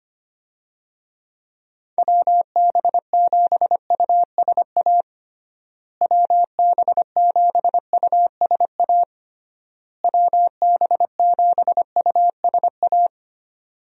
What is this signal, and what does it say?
Morse code, heard as a keyed tone, about 25 wpm, 700 Hz. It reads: WB7USA WB7USA WB7USA
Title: USA Call Signs - 6 Characters - Repeated 3x - 25wpm